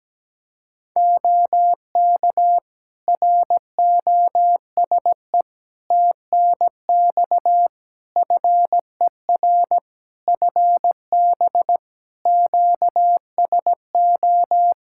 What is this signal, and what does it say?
Morse code, heard as a keyed tone, about 17 wpm, 700 Hz. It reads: OK ROSE TNX FER FB QSO
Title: Practice QSO - Exchange 4 - Ending - 17wpm